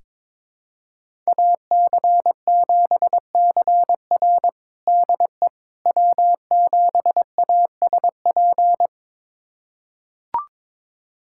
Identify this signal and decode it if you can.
Morse code, heard as a keyed tone, about 22 wpm, 700 Hz. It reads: AC7CR DE W7ASP
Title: USA Callsign DE Callsign - 5 Characters - 22wpm